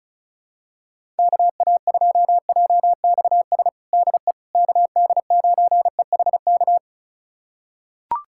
Morse code, heard as a keyed tone, about 36 wpm, 700 Hz. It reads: KA2JXS DE KD9EHK
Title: USA Callsign DE Callsign - 6 Characters - 35wpm